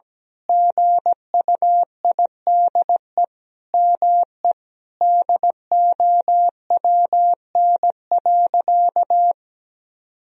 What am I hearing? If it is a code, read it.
Morse code, heard as a keyed tone, about 17 wpm, 700 Hz. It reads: GUIDE ME DOWN.
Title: Sentences from Top 1000 Words - Review Part 3 - 17wpm